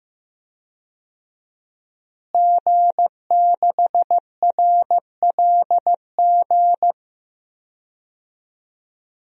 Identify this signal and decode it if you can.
Morse code, heard as a keyed tone, about 16 wpm, 700 Hz. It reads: G6RLG
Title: DXCC-100 Call Signs - 5 Characters - Repeated 3x - 15wpm